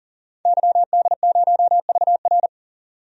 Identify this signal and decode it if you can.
Morse code, heard as a keyed tone, about 40 wpm, 700 Hz. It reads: YD0VR
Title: DXCC-100 Call Signs - 5 Characters - 40wpm